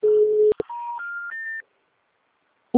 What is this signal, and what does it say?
Special information tone.
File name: out- -2048- - - .9476